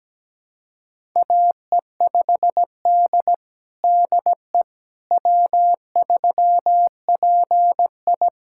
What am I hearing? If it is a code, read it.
Morse code, heard as a keyed tone, about 17 wpm, 700 Hz. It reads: AE5D DE W3PI
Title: USA Callsign DE Callsign - 4 Characters - 17wpm